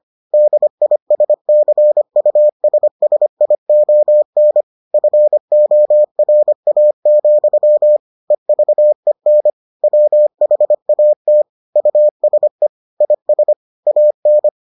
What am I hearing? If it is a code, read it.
Morse code, heard as a keyed tone, about 25 wpm, 600 Hz. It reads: DISCUSSION FORA, EVEN WHAT USE IS AN